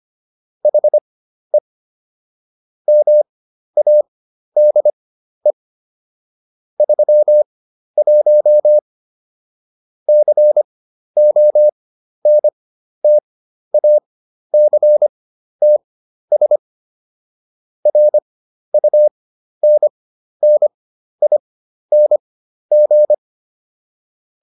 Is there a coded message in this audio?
Yes: Morse code from 0.6 to 23.1 s, heard as a keyed tone, about 25 wpm, spaced as for 12 wpm, 600 Hz. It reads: HE MADE 31 CONTACTS RUNNING